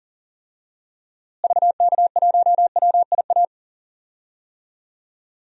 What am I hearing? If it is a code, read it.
Morse code, heard as a keyed tone, about 40 wpm, 700 Hz. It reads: VK1WIA
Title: DXCC-100 Call Signs - 6 Characters - Repeated 3x - 40wpm